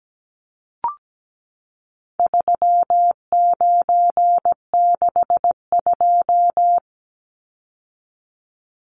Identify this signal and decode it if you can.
Morse code, heard as a keyed tone, about 17 wpm, 700 Hz. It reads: 3962
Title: Numbers - 4-Digit 17wpm